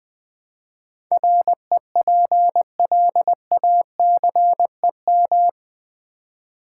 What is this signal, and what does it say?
Morse code, heard as a keyed tone, about 20 wpm, 700 Hz. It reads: REPLACEM
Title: Build A Word - 4 Letters At A Time - 11-Letter Words - 20wpm